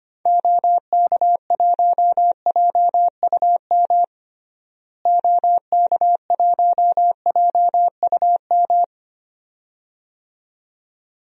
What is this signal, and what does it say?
Morse code, heard as a keyed tone, about 25 wpm, 700 Hz. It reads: OK1JUM OK1JUM
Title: DXCC-100 Call Signs - 6 Characters - Repeated 2x - 25wpm